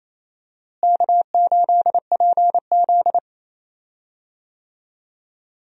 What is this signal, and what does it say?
Morse code, heard as a keyed tone, about 27 wpm, 700 Hz. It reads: K8PZ
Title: USA Call Signs - 4 Characters - Repeated 3x - 28wpm